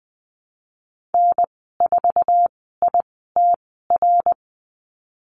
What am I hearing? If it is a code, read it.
Morse code, heard as a keyed tone, about 20 wpm, spaced as for 14 wpm, 700 Hz. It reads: N4ITR
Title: 069 Lesson 12g - Letter d - Call Signs 20wpm